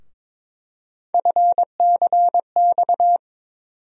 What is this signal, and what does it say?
Morse code, heard as a keyed tone, about 22 wpm, 700 Hz. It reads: FCX
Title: USA Call Sign Postfixes - 3 Letters 22wpm